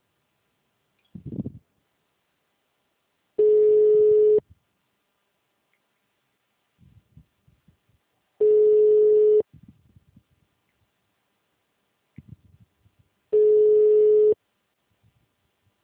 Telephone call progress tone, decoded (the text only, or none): ringback tone